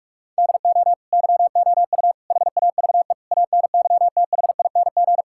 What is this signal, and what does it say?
Morse code, heard as a keyed tone, about 46 wpm, 700 Hz. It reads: DO YOU HAVE ANYTHING
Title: Sentences from Top 1000 Words - Review Part 5 - 45wpm